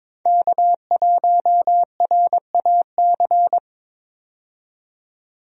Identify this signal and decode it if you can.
Morse code, heard as a keyed tone, about 21 wpm, 700 Hz. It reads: K1RAC
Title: USA Call Signs - 5 Characters - 22wpm